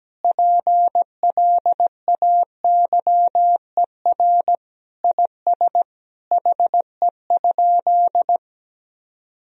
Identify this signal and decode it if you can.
Morse code, heard as a keyed tone, about 17 wpm, 700 Hz. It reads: PLAYER IS HE?
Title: Sentences from Top 1000 Words - Review Part 4 - 17wpm